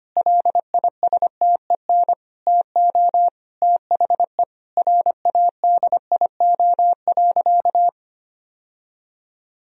Morse code, heard as a keyed tone, about 25 wpm, 700 Hz. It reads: LISTEN TO THE RADIO.
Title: Sentences from Top 1000 Words - Review Part 3 - 25wpm